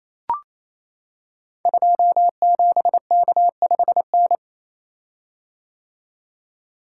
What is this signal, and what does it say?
Morse code, heard as a keyed tone, about 28 wpm, 700 Hz. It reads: 27K5N
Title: Code Groups - 5 Letter:Numbers - 28wpm